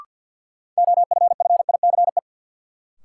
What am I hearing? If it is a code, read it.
Morse code, heard as a keyed tone, about 47 wpm, 700 Hz. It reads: OFFICE
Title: Sentences from Top 1000 Words - New Word Drill - 50wpm